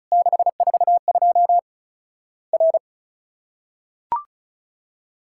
Morse code, heard as a keyed tone, about 36 wpm, 700 Hz. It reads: 642 R
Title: Worked All Europe DX Contest - QTC Exchange - 35wpm